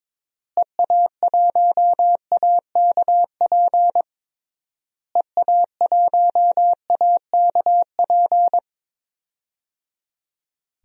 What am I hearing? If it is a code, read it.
Morse code, heard as a keyed tone, about 22 wpm, 700 Hz. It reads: EA1AKP EA1AKP
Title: DXCC-100 Call Signs - 6 Characters - Repeated 3x - 22wpm